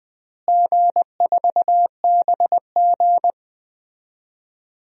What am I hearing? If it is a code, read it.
Morse code, heard as a keyed tone, about 20 wpm, 700 Hz. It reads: G4BG
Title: DXCC-100 Call Signs - 4 Characters - Repeated 2x - 20wpm